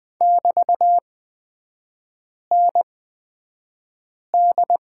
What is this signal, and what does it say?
Morse code, heard as a keyed tone, about 20 wpm, spaced as for 5 wpm, 700 Hz. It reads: <BT>ND
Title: 250 Lesson 40e - BK - Three Characters 20wpm